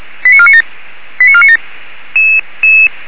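Five-tone selective call (ZVEI), heard as quick runs of five tones, tones of about 70 ms. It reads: 89497 89497
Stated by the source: ZVEI five-tone selective call 89315 received on SDR, fire brigade (Feuerwehr Orlamünde) recording.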